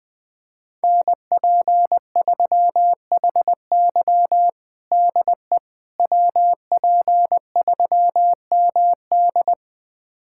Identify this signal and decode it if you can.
Morse code, heard as a keyed tone, about 20 wpm, 700 Hz. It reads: NP3HY DE WP3MD
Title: USA Callsign DE Callsign - 5 Characters - 20wpm